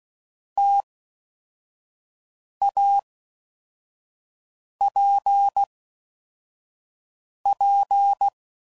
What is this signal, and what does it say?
Morse code, heard as a keyed tone, about 16 wpm, 800 Hz. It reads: T A P P
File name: Mixed / Variable Sp / T AOEIs